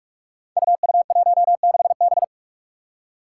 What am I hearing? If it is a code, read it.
Morse code, heard as a keyed tone, about 44 wpm, 700 Hz. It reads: UU16B
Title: Five Challenging Letter-Number Combinations 45wpm